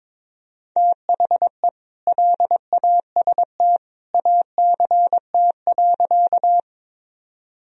Morse code, heard as a keyed tone, about 22 wpm, 700 Hz. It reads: THE LAST ACT.
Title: Sentences from Top 1000 Words - Review Part 1 - 22wpm